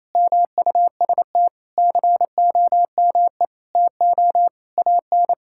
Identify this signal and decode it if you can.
Morse code, heard as a keyed tone, about 28 wpm, 700 Hz. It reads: MUST COME TO AN
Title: Sentences from Top 1000 Words - Review Part 5 - 28wpm